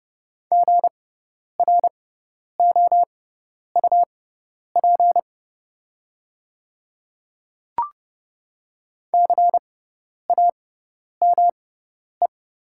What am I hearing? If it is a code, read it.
Morse code, heard as a keyed tone, about 30 wpm, spaced as for 10 wpm, 700 Hz. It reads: GROUP CAME
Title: 6x Character Spacing - Top 300 Words - 30wpm